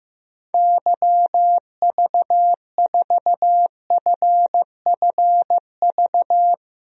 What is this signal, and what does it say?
Morse code, heard as a keyed tone, about 15 wpm, 700 Hz. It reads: YV4FFV
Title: DXCC-100 Call Signs - 6 Characters - 15wpm